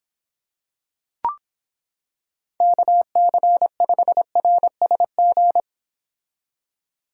Morse code, heard as a keyed tone, about 26 wpm, 700 Hz. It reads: KC5RSG